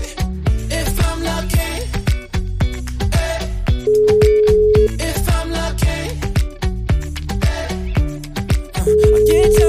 Ringback tone (Europe, 425 Hz).